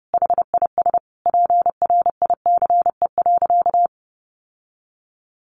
Morse code, heard as a keyed tone, about 30 wpm, 700 Hz. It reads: HIS PRICE.
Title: Sentences from Top 600 Words 30wpm